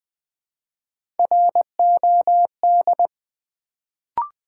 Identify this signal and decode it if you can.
Morse code, heard as a keyed tone, about 20 wpm, 700 Hz. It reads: ROD
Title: Build A Word - 1 Letter At A Time - 3-Letter Words - 20wpm